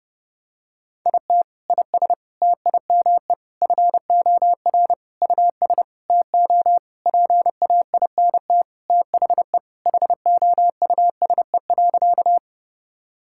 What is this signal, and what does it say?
Morse code, heard as a keyed tone, about 30 wpm, 700 Hz. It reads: IT IS TIME FOR US TO PAINT THE HOUSE.
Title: Sentences from Top 1000 Words - Review Part 6 - 30wpm